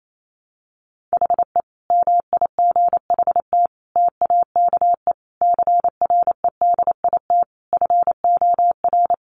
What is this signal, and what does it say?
Morse code, heard as a keyed tone, about 28 wpm, 700 Hz. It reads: HE MIGHT TAKE CREDIT FOR
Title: Sentences from Top 1000 Words 28wpm